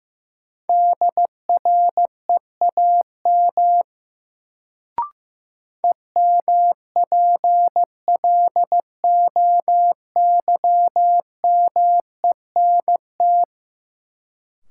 Morse code, heard as a keyed tone, about 15 wpm, 700 Hz. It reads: DREAM EMPLOYMENT
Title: Sentences from Top 1500 Words - New Word Drill - 15wpm